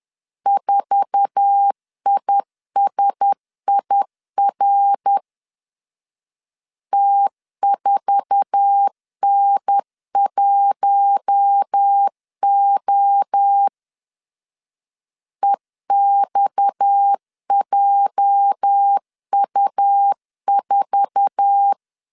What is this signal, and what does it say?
Morse code, heard as a keyed tone, about 10 wpm, 800 Hz. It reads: 4ISIR T4N1O EXJU4
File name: LEZIONI / 8 YJ45 / 8 YJ4 10wpm 2633